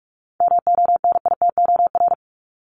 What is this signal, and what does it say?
Morse code, heard as a keyed tone, about 46 wpm, 700 Hz. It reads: MONITOR
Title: Top 1751-2000 Words 45wpm